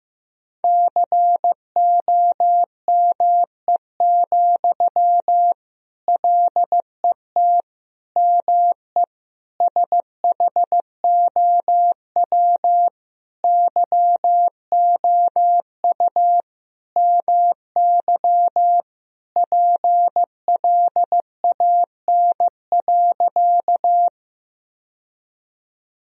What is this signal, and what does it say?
Morse code, heard as a keyed tone, about 16 wpm, 700 Hz. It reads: COME, LET ME SHOW YOU MY PLAN.
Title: Sentences from Top 1000 Words - Review Part 2 - 15wpm